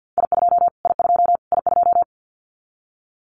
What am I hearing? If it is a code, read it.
Morse code, heard as a keyed tone, about 49 wpm, 700 Hz. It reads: I2 I2 I2
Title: Callsign Prefixes for Top 25 DXCC - Repeated 3x - 50wpm